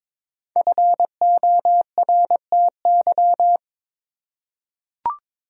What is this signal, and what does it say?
Morse code, heard as a keyed tone, about 22 wpm, 700 Hz. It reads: FORTY
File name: 5 Letter Words - First 3 Letters Given - 22wpm